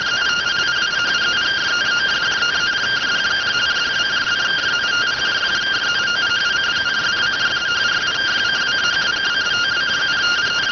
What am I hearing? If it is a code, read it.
Two-tone teleprinter signal, about 170 Hz shift.